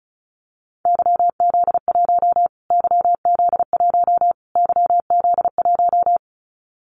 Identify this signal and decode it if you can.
Morse code, heard as a keyed tone, about 34 wpm, 700 Hz. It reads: YZ1 YZ1 YZ1
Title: Callsign Prefixes for Top 25 DXCC - Repeated 3x - 35wpm